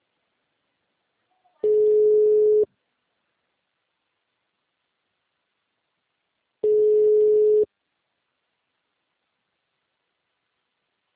Ringback tone (Europe, 425 Hz).